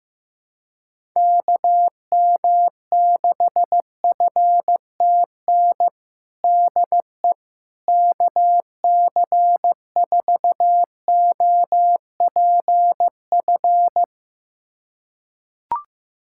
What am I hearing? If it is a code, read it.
Morse code, heard as a keyed tone, about 15 wpm, 700 Hz. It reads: KM6FTN DE KC4OPF
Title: USA Callsign DE Callsign 15wpm